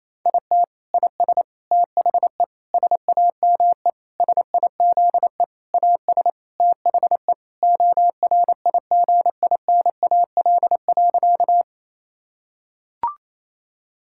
Morse code, heard as a keyed tone, about 28 wpm, 700 Hz. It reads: IT IS THE SAME SIZE AS THE ORIGINAL.
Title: Sentences from Top 1000 Words - Review Part 6 - 28wpm